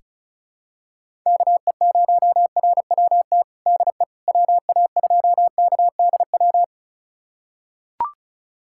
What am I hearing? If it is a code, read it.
Morse code, heard as a keyed tone, about 36 wpm, 700 Hz. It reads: KE0RWT DE WA2KDW
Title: USA Callsign DE Callsign - 6 Characters - 35wpm